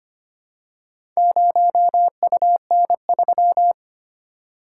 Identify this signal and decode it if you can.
Morse code, heard as a keyed tone, about 25 wpm, 700 Hz. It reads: 0UN3
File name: practice / 25wpm / Four Letter-Number Combinations - 25wpm